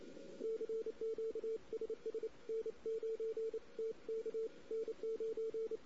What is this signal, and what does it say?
Morse code, heard as a keyed tone, about 28 wpm, 450 Hz. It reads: CQ SS N9TK N9
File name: CW / 2008-ss-cw-k5tr-op-n5rz -16:00